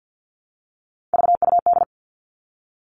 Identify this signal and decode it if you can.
Morse code, heard as a keyed tone, about 48 wpm, 700 Hz. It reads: VUD
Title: All Cut Numbers - 3-Digit 50wpm